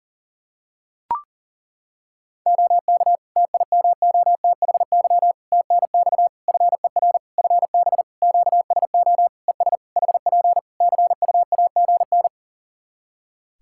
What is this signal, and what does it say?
Morse code, heard as a keyed tone, about 40 wpm, 700 Hz. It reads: OK TIMOTHY TNX FER FB QSO ES HP CUAGN
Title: K1USN SST QSO Practice 40wpm